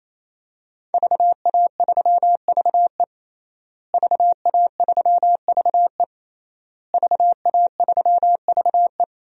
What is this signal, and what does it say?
Morse code, heard as a keyed tone, about 28 wpm, 700 Hz. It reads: VA3VE VA3VE VA3VE
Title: DXCC-100 Call Signs - 5 Characters - Repeated 3x - 28wpm